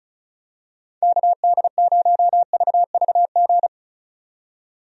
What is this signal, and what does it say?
Morse code, heard as a keyed tone, about 34 wpm, 700 Hz. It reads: KD0VVG
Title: USA Call Signs - 6 Characters - 35wpm